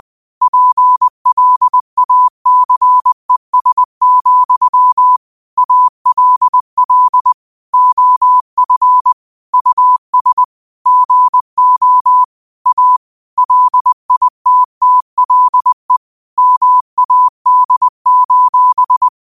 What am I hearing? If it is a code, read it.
Morse code, heard as a keyed tone, about 20 wpm, 1000 Hz. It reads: PLACES, ALL OF US GO A LITTLE MAD:
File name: qotdfiles / 20wpm